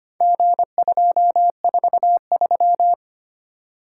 Morse code, heard as a keyed tone, about 25 wpm, 700 Hz. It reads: G243
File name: practice / 25wpm / Four Letter-Number Combinations - 25wpm